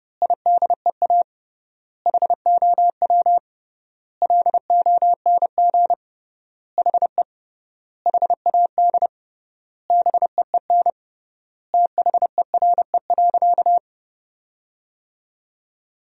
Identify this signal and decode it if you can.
Morse code, heard as a keyed tone, about 30 wpm, 700 Hz. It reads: IDEA HOW LONG HE HAD BEEN THERE.
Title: 3x Word-Spacing - Sentences from Top 300 Words - 30wpm